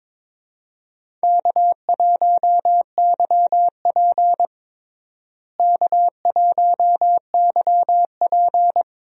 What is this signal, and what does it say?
Morse code, heard as a keyed tone, about 21 wpm, 700 Hz. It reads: K1YP K1YP
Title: USA Call Signs - 4 Characters - Repeated 3x - 22wpm